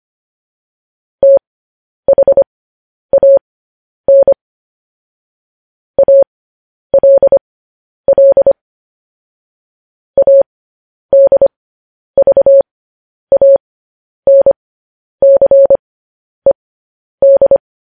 Morse code, heard as a keyed tone, about 25 wpm, spaced as for 10 wpm, 550 Hz. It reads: THAN ALL ADVANCED
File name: podcast / News headlines-1x-25x10